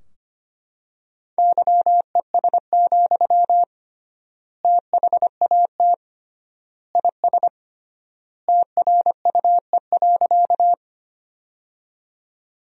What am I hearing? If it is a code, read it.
Morse code, heard as a keyed tone, about 25 wpm, 700 Hz. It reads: YES, THAT IS TRUE.
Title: 3x Word-Spacing - Sentences from Top 500 Words - 25wpm